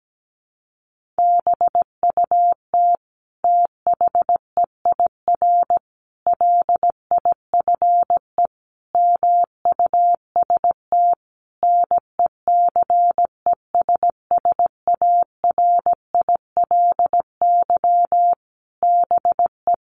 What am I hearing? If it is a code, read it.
Morse code, heard as a keyed tone, about 17 wpm, 700 Hz. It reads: BUT THEIR LIFE MUST NECESSARILY BE
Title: Sentences from Top 1400 Words 17wpm